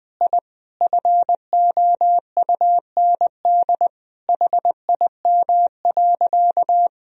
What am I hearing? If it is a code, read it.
Morse code, heard as a keyed tone, about 21 wpm, 700 Hz. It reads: I FOUND HIM.
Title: Sentences from Top 1000 Words - Review Part 3 - 20wpm